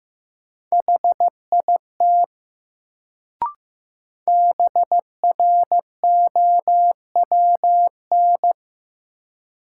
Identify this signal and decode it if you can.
Morse code, heard as a keyed tone, about 15 wpm, 700 Hz. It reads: HIT BROWN
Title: Sentences from Top 1200 Words - New Word Drill Alt - 15wpm